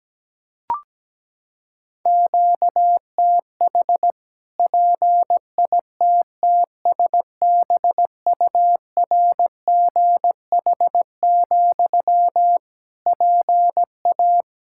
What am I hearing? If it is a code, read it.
Morse code, heard as a keyed tone, about 17 wpm, 700 Hz. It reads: QTH PITTSBURGH, PA
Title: Practice QSO - Exchange 1 - Beginning - 17wpm